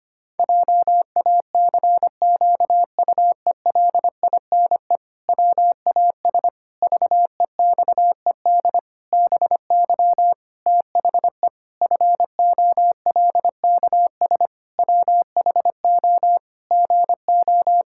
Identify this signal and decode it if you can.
Morse code, heard as a keyed tone, about 25 wpm, 700 Hz. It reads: JACQUELINE WAS VEXED BY THE FOLKS WHO GO
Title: Pangram Challenge v2 - 25wpm